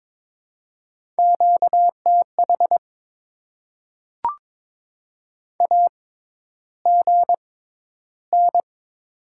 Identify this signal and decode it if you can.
Morse code, heard as a keyed tone, about 22 wpm, spaced as for 11 wpm, 690 Hz. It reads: QTH AGN
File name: Top 20 QSO Elements - 6x Character Spacing - 22wpm